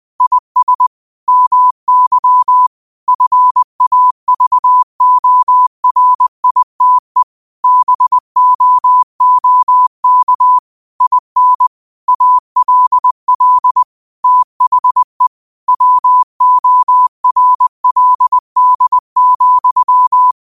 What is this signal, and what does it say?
Morse code, heard as a keyed tone, about 20 wpm, 1000 Hz. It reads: IS MY FAVORITE BOOK IN ALL THE WORLD,
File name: qotdfiles / 20wpm